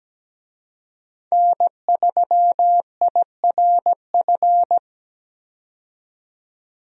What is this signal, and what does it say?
Morse code, heard as a keyed tone, about 17 wpm, 700 Hz. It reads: N3IRF
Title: USA Call Signs - 5 Characters - 17wpm